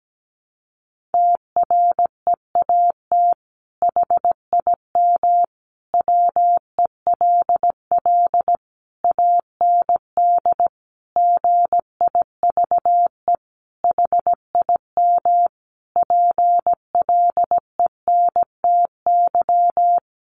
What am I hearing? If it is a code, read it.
Morse code, heard as a keyed tone, about 17 wpm, 700 Hz. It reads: TREAT HIM WELL AND GIVE HIM PLENTY
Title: Sentences from Top 1300 Words 17wpm